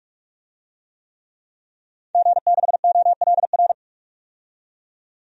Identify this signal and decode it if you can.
Morse code, heard as a keyed tone, about 41 wpm, 700 Hz. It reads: G6OLR
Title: DXCC-100 Call Signs - 5 Characters - 45wpm